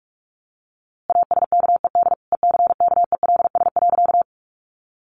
Morse code, heard as a keyed tone, about 44 wpm, 700 Hz. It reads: ASKED ECKELS.
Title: A Sound Of Thunder 45wpm